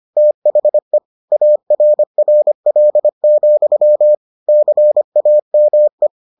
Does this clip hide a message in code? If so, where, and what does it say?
Yes: Morse code from 0.2 to 6.1 s, heard as a keyed tone, about 25 wpm, 600 Hz. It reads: THE ARRL, CAME